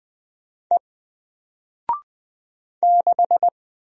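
Morse code, heard as a keyed tone, about 20 wpm, 700 Hz. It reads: E 6
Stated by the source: Single letter and number Morse drill at 20 wpm, from N4LCD.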